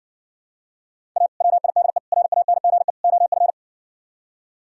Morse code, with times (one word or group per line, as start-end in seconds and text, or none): 1.2-1.3 A
1.4-2.0 WIDE
2.1-2.9 RANGE
3.0-3.5 OF